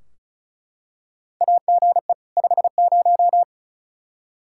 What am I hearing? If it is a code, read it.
Morse code, heard as a keyed tone, about 36 wpm, 700 Hz. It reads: AGE 50
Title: Practice QSO - Exchange 3 - More Info - 35wpm